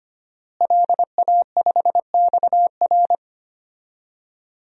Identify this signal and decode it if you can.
Morse code, heard as a keyed tone, about 25 wpm, 700 Hz. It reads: LA5XR
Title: DXCC-100 Call Signs - 5 Characters - Repeated 3x - 25wpm